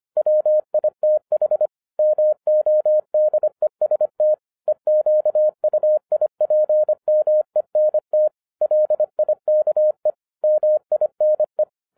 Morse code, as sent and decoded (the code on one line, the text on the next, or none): .-- .. - .... / -- --- -.. ... - / . --.- ..- .. .--. -- . -. - / .-.. .. -.- . / -- .. -. .
WITH MODEST EQUIPMENT LIKE MINE